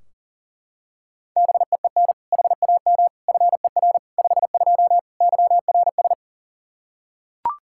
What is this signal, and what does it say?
Morse code, heard as a keyed tone, about 37 wpm, 700 Hz. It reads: BEEN HAM FER 52 YRS
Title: Practice QSO - Exchange 3 - More Info - 40wpm